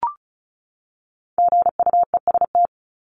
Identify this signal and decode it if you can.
Morse code, heard as a keyed tone, about 34 wpm, 700 Hz. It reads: GUEST
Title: Top 1501-1750 Words 35wpm